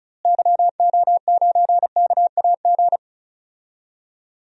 Morse code, heard as a keyed tone, about 36 wpm, 700 Hz. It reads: YO9KAG
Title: DXCC-100 Call Signs - 6 Characters - Repeated 3x - 35wpm